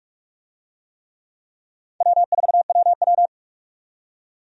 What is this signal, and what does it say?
Morse code, heard as a keyed tone, about 46 wpm, 700 Hz. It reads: W4WW
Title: USA Call Signs - 4 Characters - 45wpm